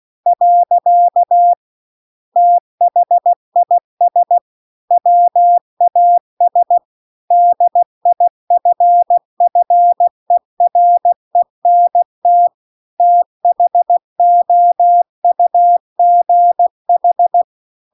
Morse code, heard as a keyed tone, about 16 wpm, 700 Hz. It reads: . THIS WAS DIFFERENT THOUGH